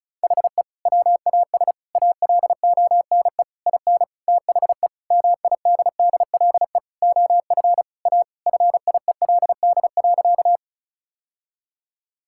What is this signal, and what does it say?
Morse code, heard as a keyed tone, about 36 wpm, 700 Hz. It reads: HE WAS ALONE IN THE MIDDLE OF A FIELD.
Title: Sentences from Top 1000 Words - Review Part 6 - 35wpm